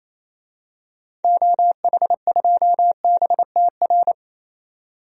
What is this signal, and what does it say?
Morse code, heard as a keyed tone, about 28 wpm, 700 Hz. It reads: OH2BTR
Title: DXCC-100 Call Signs - 6 Characters - 28wpm